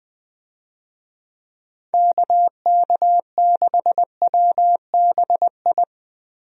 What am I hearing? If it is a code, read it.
Morse code, heard as a keyed tone, about 21 wpm, 700 Hz. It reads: KK6WBI
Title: USA Call Signs - 6 Characters - Repeated 3x - 20wpm